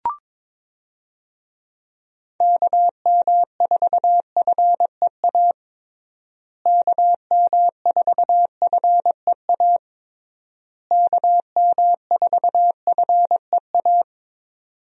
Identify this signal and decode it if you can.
Morse code, heard as a keyed tone, about 22 wpm, 700 Hz. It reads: KM4FEA KM4FEA KM4FEA
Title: USA Call Signs - 6 Characters - Repeated 3x - 22wpm